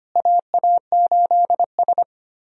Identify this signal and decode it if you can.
Morse code, heard as a keyed tone, about 25 wpm, 700 Hz. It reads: AA8S
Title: USA Call Signs - 4 Characters - Repeated 3x - 25wpm